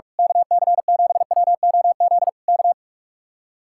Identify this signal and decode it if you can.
Morse code, heard as a keyed tone, about 45 wpm, 700 Hz. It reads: KC7WOZ K
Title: Speed-Racing Common QSO Phrases - 45wpm